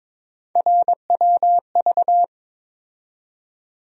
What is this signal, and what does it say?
Morse code, heard as a keyed tone, about 22 wpm, 700 Hz. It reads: RWV
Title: USA Call Sign Postfixes - 3 Letters 22wpm